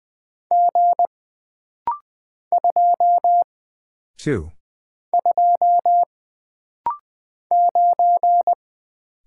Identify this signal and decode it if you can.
Morse code, heard as a keyed tone, about 20 wpm, 700 Hz. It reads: G 2 2 9
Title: Single Character (Letter:Number:Punctuation:Prosign) - Rapid-Fire - 20wpm